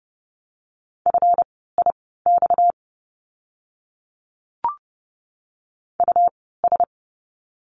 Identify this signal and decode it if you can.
Morse code, heard as a keyed tone, about 30 wpm, spaced as for 17 wpm, 700 Hz. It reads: FIX US
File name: course / 30:10wpm / 230 Lesson 37f - Letter x - Words 30wpm